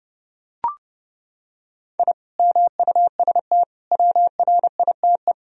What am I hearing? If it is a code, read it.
Morse code, heard as a keyed tone, about 30 wpm, 700 Hz. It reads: I MUST WRITE
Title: Sentences from Top 1000 Words - Review Part 1 - 30wpm